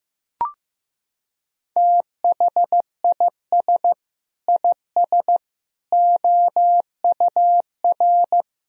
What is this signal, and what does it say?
Morse code, heard as a keyed tone, about 15 wpm, 700 Hz. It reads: THIS IS OUR
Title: Sentences from Top 1000 Words - Review Part 2 - 15wpm